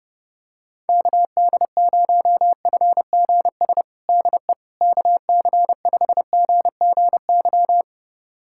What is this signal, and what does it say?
Morse code, heard as a keyed tone, about 30 wpm, 700 Hz. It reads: KD0FGS DE KC5GGY
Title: USA Callsign DE Callsign - 6 Characters - 30wpm